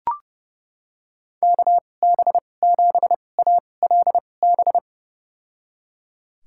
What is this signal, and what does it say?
Morse code, heard as a keyed tone, about 30 wpm, spaced as for 22 wpm, 700 Hz. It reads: KB7ALB
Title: USA Call Signs - 6 Characters - 2x Character Spacing - 30wpm